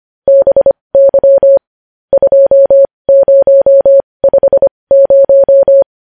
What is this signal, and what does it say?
Morse code, heard as a keyed tone, about 25 wpm, spaced as for 20 wpm, 550 Hz. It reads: BY 2050
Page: news-headlines-in-morse-code-at-20-wpm